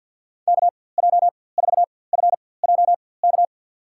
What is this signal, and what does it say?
Morse code, heard as a keyed tone, about 51 wpm, spaced as for 23 wpm, 700 Hz. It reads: KJ4FJK
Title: USA Call Signs - 6 Characters - 4x Character Spacing - 50wpm